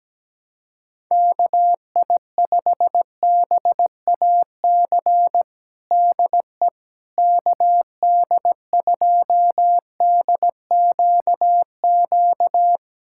Morse code, heard as a keyed tone, about 17 wpm, 700 Hz. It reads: KI5BAC DE KD2DQQ
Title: USA Callsign DE Callsign - 6 Characters - 17wpm